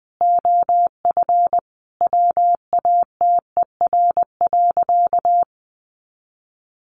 Morse code, heard as a keyed tone, about 20 wpm, 700 Hz. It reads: OF WATER.
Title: Additional Easy Phrases 20wpm